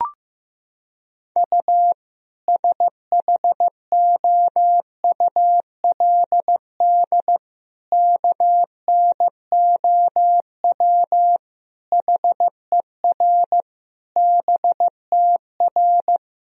Morse code, heard as a keyed tone, about 15 wpm, 700 Hz. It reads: U SHOULD KNOW HER BTR
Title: Sentences from Top 400 Words and Abbreviations 15wpm